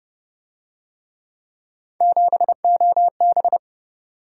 Morse code, heard as a keyed tone, about 27 wpm, 700 Hz. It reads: 7OB